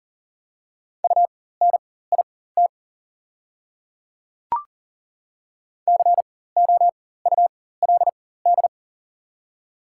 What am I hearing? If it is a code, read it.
Morse code, heard as a keyed tone, about 40 wpm, spaced as for 19 wpm, 690 Hz. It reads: UNIT COULD
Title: 4x Character Spacing - Top 500 Words - 40wpm